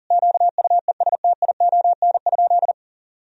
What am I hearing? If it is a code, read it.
Morse code, heard as a keyed tone, about 40 wpm, 700 Hz. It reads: QUESTION?
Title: Sentences from Top 400 Words and Abbreviations 40wpm